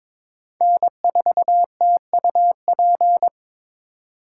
Morse code, heard as a keyed tone, about 26 wpm, 700 Hz. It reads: N4TUP
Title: Speed Racing - USA 1x3 Call Signs 22wpm